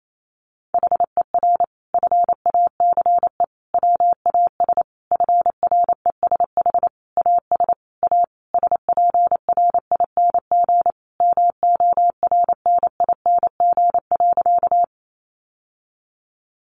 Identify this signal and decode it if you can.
Morse code, heard as a keyed tone, about 28 wpm, 700 Hz. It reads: HER FACE WAS FRESH AS A SPRING MORNING.
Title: Sentences from Top 1200 Words 28wpm